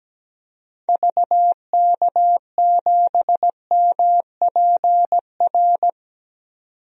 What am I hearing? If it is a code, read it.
Morse code, heard as a keyed tone, about 17 wpm, 700 Hz. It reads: VK7MPR
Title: DXCC-100 Call Signs - 6 Characters - 17wpm